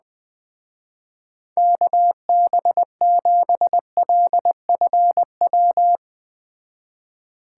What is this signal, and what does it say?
Morse code, heard as a keyed tone, about 21 wpm, 700 Hz. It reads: KB7LFW